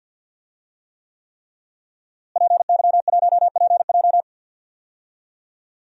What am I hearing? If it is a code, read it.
Morse code, heard as a keyed tone, about 52 wpm, 700 Hz. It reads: PY1PJ
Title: DXCC-100 Call Signs - 5 Characters - Repeated 3x - 50wpm